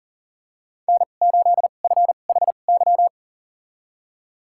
Morse code, heard as a keyed tone, about 40 wpm, spaced as for 29 wpm, 700 Hz. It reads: N8FHY
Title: USA Call Signs - 5 Characters - 2x Character Spacing - 40wpm